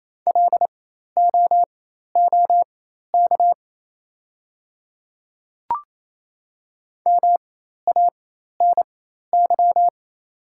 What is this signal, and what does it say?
Morse code, heard as a keyed tone, about 27 wpm, spaced as for 13 wpm, 690 Hz. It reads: LOOK MANY
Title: 4x Character Spacing - Top 100 Words - 28wpm